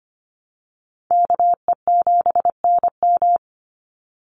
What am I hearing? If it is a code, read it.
Morse code, heard as a keyed tone, about 25 wpm, 700 Hz. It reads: KE7NM
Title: USA Call Signs 25wpm